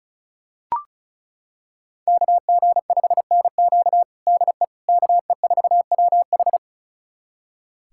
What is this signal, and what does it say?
Morse code, heard as a keyed tone, about 34 wpm, 700 Hz. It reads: KG5NQ DE KE4WH
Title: USA Callsign DE Callsign 35wpm